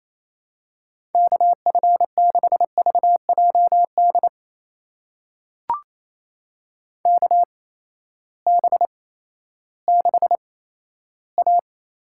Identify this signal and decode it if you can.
Morse code, heard as a keyed tone, about 28 wpm, spaced as for 14 wpm, 700 Hz. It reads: KF6VJD KB6A
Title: USA Call Signs - 6 Characters - 8x Character Spacing - 28wpm